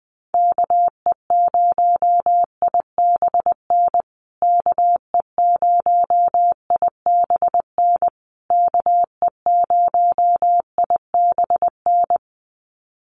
Morse code, heard as a keyed tone, about 20 wpm, 700 Hz. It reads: KE0IBN KE0IBN KE0IBN
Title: USA Call Signs Repeated 3x 20wpm